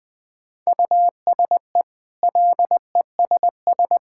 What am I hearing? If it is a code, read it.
Morse code, heard as a keyed tone, about 20 wpm, 700 Hz. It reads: USE LESS
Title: Easy Sentences with Hints - 20wpm